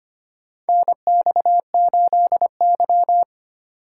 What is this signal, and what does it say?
Morse code, heard as a keyed tone, about 25 wpm, 700 Hz. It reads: NX8Y
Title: USA Call Signs - 4 Characters - 3x Character Spacing - 25wpm